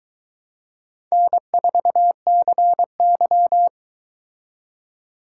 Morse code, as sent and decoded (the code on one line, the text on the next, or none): -. ....- -.-. -.--
N4CY